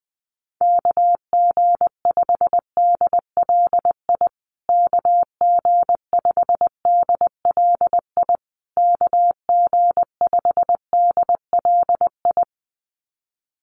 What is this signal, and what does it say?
Morse code, heard as a keyed tone, about 20 wpm, 700 Hz. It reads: KG5DLI KG5DLI KG5DLI
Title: USA Call Signs Repeated 3x 20wpm